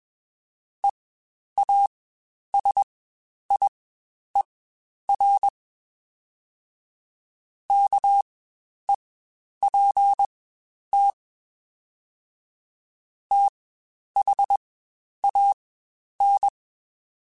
Morse code, heard as a keyed tone, about 21 wpm, spaced as for 10 wpm, 800 Hz. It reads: EASIER KEPT THAN
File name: Quotes Spaced / Quotes Spaced-mp3s / Character sp @20